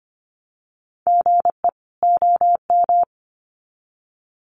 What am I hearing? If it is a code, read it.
Morse code, heard as a keyed tone, about 25 wpm, 700 Hz. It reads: GE OM
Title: Common QSO Phrases 25wpm